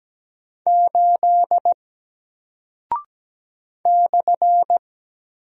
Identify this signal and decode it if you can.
Morse code, heard as a keyed tone, about 17 wpm, 700 Hz. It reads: 8 /
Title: Single Character (Letter:Number:Punctuation:Prosign) Alt 17wpm